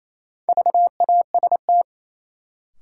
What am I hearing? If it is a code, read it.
Morse code, heard as a keyed tone, about 27 wpm, 700 Hz. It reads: VAST
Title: Sentences from Top 1300 Words - New Word Drill - 28wpm